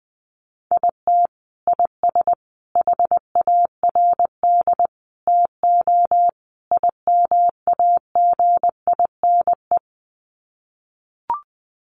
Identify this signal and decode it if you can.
Morse code, heard as a keyed tone, about 19 wpm, 700 Hz. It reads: IT IS HARD TO IMAGINE
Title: Sets of 5 Words Encore2 20wpm